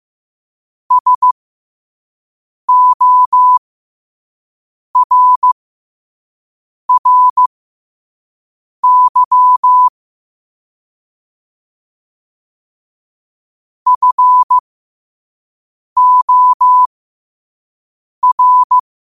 Morse code, heard as a keyed tone, about 15 wpm, spaced as for 5 wpm, 1000 Hz. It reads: SORRY FOR
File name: qotdfiles / 5wpm